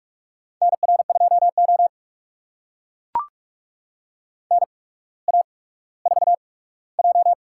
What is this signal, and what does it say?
Morse code, heard as a keyed tone, about 44 wpm, spaced as for 19 wpm, 690 Hz. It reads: NR2O NA4J
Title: USA Call Signs - 4 Characters - 8x Character Spacing - 45wpm